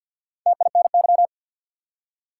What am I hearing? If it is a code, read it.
Morse code, heard as a keyed tone, about 47 wpm, 700 Hz. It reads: TINY
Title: Four Letter Words - 50wpm